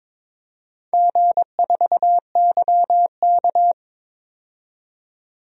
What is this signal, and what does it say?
Morse code, heard as a keyed tone, about 21 wpm, 700 Hz. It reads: G4YK
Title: DXCC-100 Call Signs - 4 Characters - 22wpm